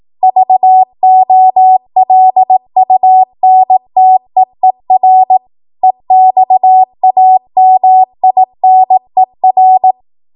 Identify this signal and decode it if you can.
Morse code, heard as a keyed tone, about 18 wpm, 750 Hz. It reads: VOLUNTEER EXAMINER